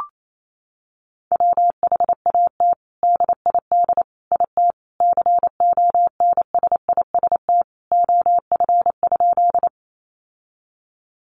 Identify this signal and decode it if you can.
Morse code, heard as a keyed tone, about 28 wpm, 700 Hz. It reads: WHAT DID IT CONSIST OF?
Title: Sentences from Top 1400 Words 28wpm